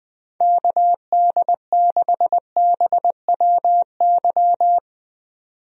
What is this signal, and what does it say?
Morse code, heard as a keyed tone, about 20 wpm, 700 Hz. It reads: KD6BWY
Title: USA Call Signs - 6 Characters - 20wpm